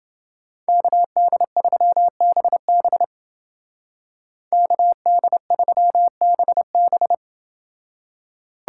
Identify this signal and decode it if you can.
Morse code, heard as a keyed tone, about 29 wpm, 700 Hz. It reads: KD3BB KD3BB